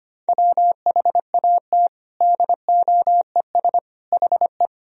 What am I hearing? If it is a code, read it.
Morse code, heard as a keyed tone, about 25 wpm, 700 Hz. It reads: WHAT DOES HE
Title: Sentences from Top 1000 Words - Review Part 1 - 25wpm